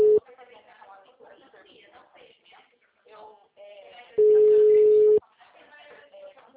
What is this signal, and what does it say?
Ringback tone (Europe, 425 Hz), with speech.
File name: out- -2017- - - .27746